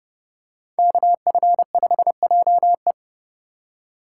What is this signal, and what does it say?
Morse code, heard as a keyed tone, about 30 wpm, 700 Hz. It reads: KF5JE
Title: USA Call Signs - 5 Characters - 3x Character Spacing - 30wpm